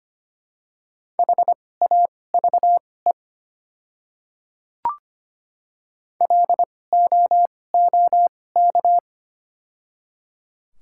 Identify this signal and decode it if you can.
Morse code, heard as a keyed tone, about 25 wpm, spaced as for 18 wpm, 700 Hz. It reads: HAVE LOOK
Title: 2x Character Spacing - Top 100 Words - 25wpm